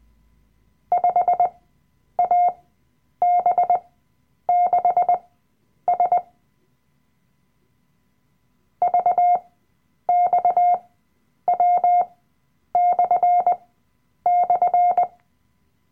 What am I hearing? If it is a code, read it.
Morse code, heard as a keyed tone, about 20 wpm, spaced as for 9 wpm, 710 Hz. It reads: 5AB6S VXW//
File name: Workshop Dateien / CW-Online-Kurs / Lektion 007